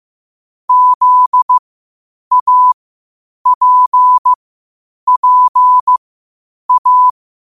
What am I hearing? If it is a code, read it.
Morse code, heard as a keyed tone, about 15 wpm, spaced as for 8 wpm, 1000 Hz. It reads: ZAPPA
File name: 8wpm